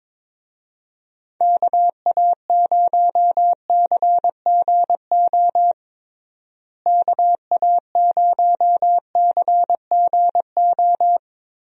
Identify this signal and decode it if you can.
Morse code, heard as a keyed tone, about 22 wpm, 700 Hz. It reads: KA0CGO KA0CGO
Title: USA Call Signs - 6 Characters - Repeated 3x - 22wpm